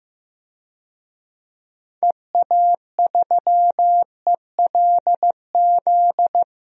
Morse code, heard as a keyed tone, about 16 wpm, 700 Hz. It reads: EA3ELZ
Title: DXCC-100 Call Signs - 6 Characters - Repeated 2x - 15wpm